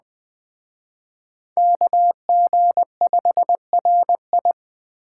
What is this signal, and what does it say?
Morse code, heard as a keyed tone, about 20 wpm, 700 Hz. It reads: KG5RI
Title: Speed Racing - USA 2x2 Call Signs 17wpm